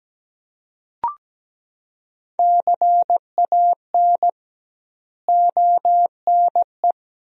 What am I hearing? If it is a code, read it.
Morse code, heard as a keyed tone, about 17 wpm, 700 Hz. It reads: CAN ONE
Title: 2x Word-Spacing - Sentences from Top 100 Words - 17wpm